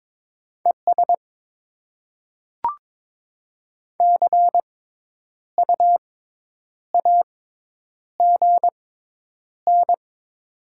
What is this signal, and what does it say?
Morse code, heard as a keyed tone, about 22 wpm, spaced as for 9 wpm, 700 Hz. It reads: ES CU AGN
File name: Top 20 QSO Elements - 6x Character Spacing - 22wpm